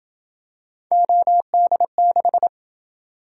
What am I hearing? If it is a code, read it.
Morse code, heard as a keyed tone, about 26 wpm, 700 Hz. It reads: OD6